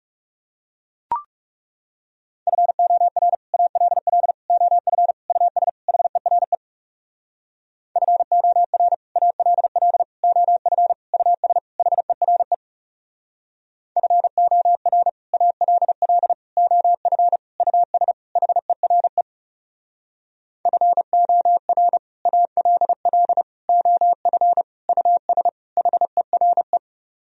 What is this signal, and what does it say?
Morse code, heard as a keyed tone, about 40 wpm, 700 Hz. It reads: FOR ALL OF US HERE FOR ALL OF US HERE FOR ALL OF US HERE FOR ALL OF US HERE